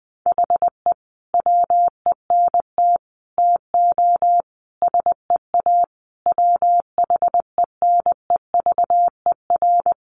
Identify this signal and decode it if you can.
Morse code, heard as a keyed tone, about 19 wpm, 700 Hz. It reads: HE WENT TO SEA WHENEVER
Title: Sentences from Top 1200 Words 20wpm